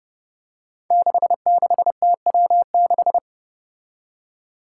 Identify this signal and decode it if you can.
Morse code, heard as a keyed tone, about 30 wpm, 700 Hz. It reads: 66TW6
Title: Five Challenging Letter-Number Combinations 30wpm